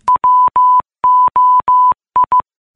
Morse code, heard as a keyed tone, about 15 wpm, 1000 Hz. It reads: WOI